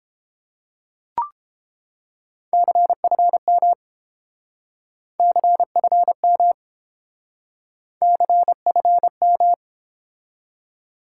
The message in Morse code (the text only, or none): CFM CFM CFM